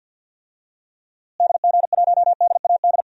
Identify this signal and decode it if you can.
Morse code, heard as a keyed tone, about 53 wpm, 700 Hz. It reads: DG1DAD
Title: DXCC-100 Call Signs - 6 Characters - Repeated 2x - 50wpm